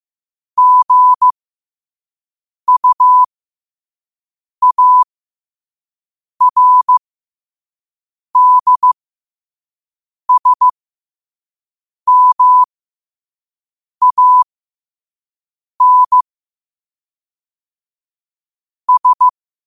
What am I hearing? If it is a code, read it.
Morse code, heard as a keyed tone, about 15 wpm, spaced as for 5 wpm, 1000 Hz. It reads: GUARDSMAN S
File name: qotdfiles / 5wpm